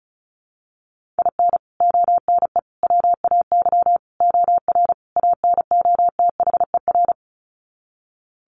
Morse code, heard as a keyed tone, about 34 wpm, 700 Hz. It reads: IN ONE WAY OR ANOTHER
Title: Sets of 5 Words Encore 35wpm